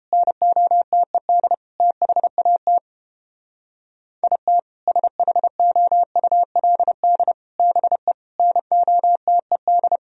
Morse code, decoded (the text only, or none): NOTED THAT IT SHOULD BE NOTED